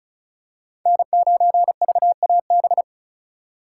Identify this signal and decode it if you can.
Morse code, heard as a keyed tone, about 36 wpm, 690 Hz. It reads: N9VAB
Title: USA Call Signs - 5 Characters - 3x Character Spacing - 35wpm